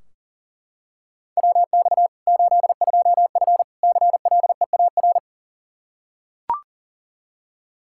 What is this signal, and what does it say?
Morse code, heard as a keyed tone, about 40 wpm, 700 Hz. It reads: WX 82F CLEAR
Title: Practice QSO - Exchange 2 - Info - 40wpm